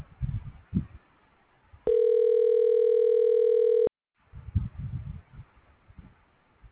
Ringback tone (North America), 440 + 480 Hz.